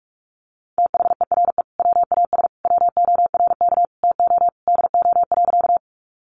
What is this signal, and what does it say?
Morse code, heard as a keyed tone, about 44 wpm, 700 Hz. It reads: THERE WAS WORK TO DO.
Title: Sentences from Top 200 Words 45wpm